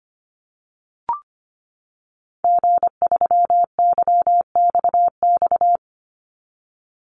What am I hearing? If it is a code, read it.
Morse code, heard as a keyed tone, about 25 wpm, 700 Hz. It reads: G3YXX
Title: Call Signs - 25wpm